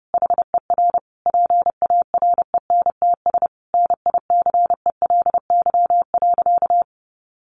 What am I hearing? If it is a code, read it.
Morse code, heard as a keyed tone, about 30 wpm, 700 Hz. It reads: HER PARENTS NICELY.
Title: Easy Phrases 30wpm